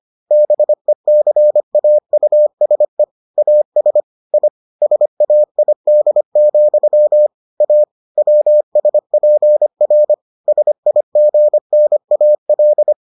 Morse code, heard as a keyed tone, about 25 wpm, 600 Hz. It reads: BECAUSE AS I SAID, A WSPR SIGNAL